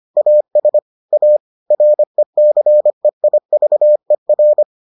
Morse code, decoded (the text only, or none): AS A RECEIVER